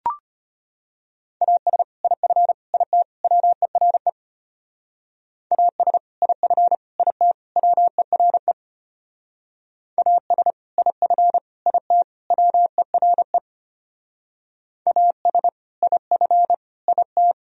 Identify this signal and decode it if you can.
Morse code, heard as a keyed tone, about 32 wpm, 700 Hz. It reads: AS IF IT WERE AS IF IT WERE AS IF IT WERE AS IF IT